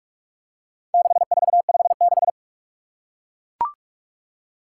Morse code, heard as a keyed tone, about 41 wpm, 700 Hz. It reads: 6456
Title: Four Challenging Letter-Number Combinations 45wpm